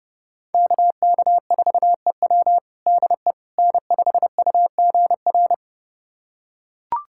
Morse code, heard as a keyed tone, about 30 wpm, 700 Hz. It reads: KK4EW DE N5UGR
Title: USA Callsign DE Callsign - 5 Characters - 30wpm